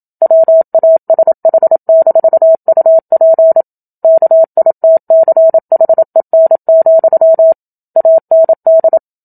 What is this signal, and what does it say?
Morse code, heard as a keyed tone, about 27 wpm, 650 Hz. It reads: WASH-UP KITCHEN, AND
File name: Book Two / 01-Chapter 1 - 27 WPM